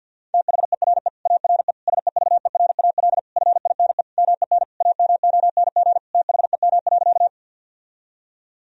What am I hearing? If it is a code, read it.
Morse code, heard as a keyed tone, about 47 wpm, 700 Hz. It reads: THERE ARE SEVERAL FINE MEN AMONG THEM.
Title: Sentences from Top 1000 Words - Review Part 2 - 50wpm